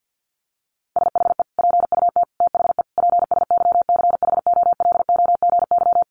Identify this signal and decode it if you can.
Morse code, heard as a keyed tone, about 50 wpm, 700 Hz. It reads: SHE PUT THE PSYCHOLOGY
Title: Sentences from Top 1400 Words 50wpm